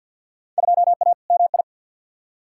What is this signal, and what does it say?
Morse code, heard as a keyed tone, about 48 wpm, 700 Hz. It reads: 2A MI
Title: Field Day Exchanges Encore - 50wpm